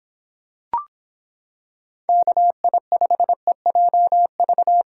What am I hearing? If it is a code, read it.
Morse code, heard as a keyed tone, about 25 wpm, 700 Hz. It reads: KI5EJV